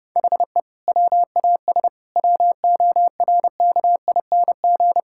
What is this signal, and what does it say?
Morse code, heard as a keyed tone, about 30 wpm, 700 Hz. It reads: HE WAS WORKING
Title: Sentences from Top 1000 Words - Review Part 5 - 30wpm